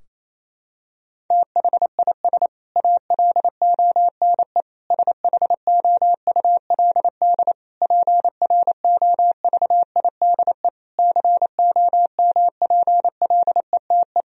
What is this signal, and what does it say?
Morse code, heard as a keyed tone, about 27 wpm, 700 Hz. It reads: THIS ALONE SHOULD PROVIDE COMPLETE
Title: Sentences from Top 1000 Words - Review Part 4 - 28wpm